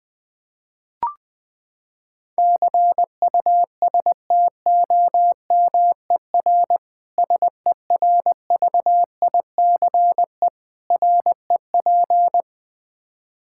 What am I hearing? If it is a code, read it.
Morse code, heard as a keyed tone, about 20 wpm, 700 Hz. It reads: CUSTOMER SERVICE REP
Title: Practice QSO - Exchange 3 - More Info - 20wpm